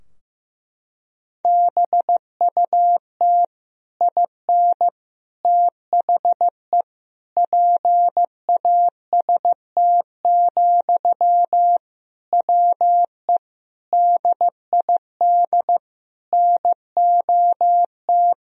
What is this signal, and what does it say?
Morse code, heard as a keyed tone, about 15 wpm, 700 Hz. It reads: BUT IN THE PAST, WE DID NOT
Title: Sentences from Top 1000 Words - Review Part 3 - 15wpm